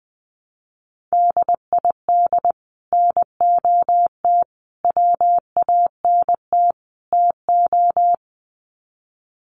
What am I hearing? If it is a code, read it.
Morse code, heard as a keyed tone, about 20 wpm, 700 Hz. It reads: DID NOT WANT TO
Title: Sets of 4 Words 20wpm